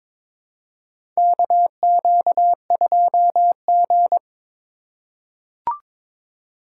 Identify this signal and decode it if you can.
Morse code, heard as a keyed tone, about 22 wpm, 700 Hz. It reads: KQ2G
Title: USA Call Signs - 4 Characters - 2x Character Spacing - 22wpm